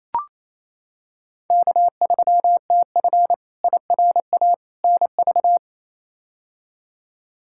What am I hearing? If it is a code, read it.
Morse code, heard as a keyed tone, about 28 wpm, 700 Hz. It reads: K3TF IRA NV
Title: K1USN SST QSO Practice - 28wpm